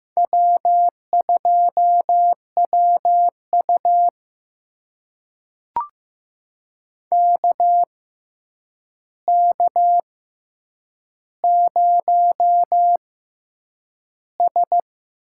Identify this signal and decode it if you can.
Morse code, heard as a keyed tone, about 15 wpm, spaced as for 8 wpm, 700 Hz. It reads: W2WU KK0S
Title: USA Call Signs - 4 Characters - 6x Character Spacing - 15wpm